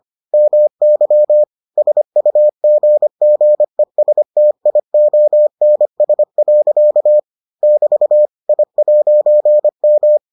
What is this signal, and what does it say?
Morse code, heard as a keyed tone, about 25 wpm, 600 Hz. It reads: MY SUGGESTIONS. <BT> I'M